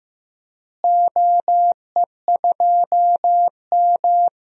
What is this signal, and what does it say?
Morse code, heard as a keyed tone, about 16 wpm, 700 Hz. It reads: OE2M
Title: DXCC-100 Call Signs - 4 Characters - Repeated 3x - 15wpm